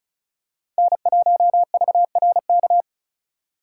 Morse code, heard as a keyed tone, about 36 wpm, 700 Hz. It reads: N1VRK
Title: USA Call Signs - 5 Characters - 2x Character Spacing - 35wpm